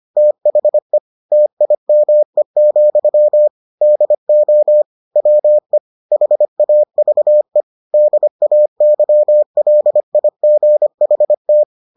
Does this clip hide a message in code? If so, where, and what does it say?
Yes: Morse code from 0.2 to 11.6 s, heard as a keyed tone, about 25 wpm, 600 Hz. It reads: THE TIME, DO WE HAVE DAYLIGHT